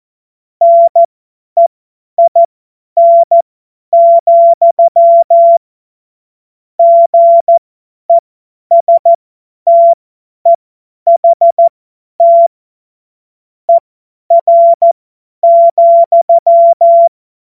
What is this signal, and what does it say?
Morse code, heard as a keyed tone, about 14 wpm, spaced as for 10 wpm, 680 Hz. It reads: NEIN, GESTEHT ER,
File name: Witz-12-August-2025-50-F70